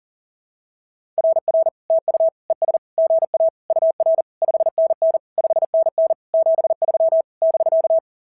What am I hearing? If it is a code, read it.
Morse code, heard as a keyed tone, about 39 wpm, 690 Hz. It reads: RR TU ES GA UR 5NN 5NN 73 <BK>
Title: SOTA QSO Practice 40wpm